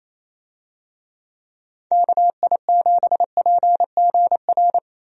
Morse code, heard as a keyed tone, about 28 wpm, 700 Hz. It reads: KI7PGR
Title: USA Call Signs - 6 Characters - Repeated 3x - 28wpm